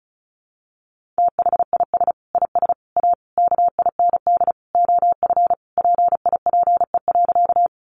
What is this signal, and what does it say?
Morse code, heard as a keyed tone, about 34 wpm, 700 Hz. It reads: THIS IS A KIND OF PIPE.
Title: Sentences from Top 1500 Words 35wpm